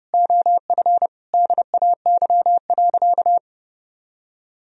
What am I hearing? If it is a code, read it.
Morse code, heard as a keyed tone, about 28 wpm, 700 Hz. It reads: OF DAY.
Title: Sentences from Top 1000 Words - Review Part 4 - 30wpm